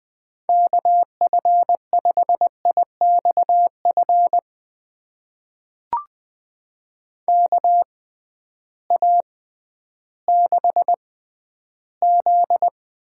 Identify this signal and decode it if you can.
Morse code, heard as a keyed tone, about 20 wpm, spaced as for 12 wpm, 700 Hz. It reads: KF5IXF KA6Z
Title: USA Call Signs - 6 Characters - 6x Character Spacing - 20wpm